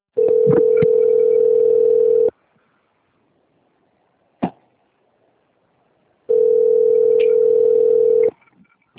Ringback tone (North America), 440 + 480 Hz.
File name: out- - - - - .20864